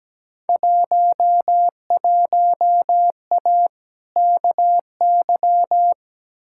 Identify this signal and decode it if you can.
Morse code, heard as a keyed tone, about 17 wpm, 700 Hz. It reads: 11A KY
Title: Field Day Exchanges Encore - 17wpm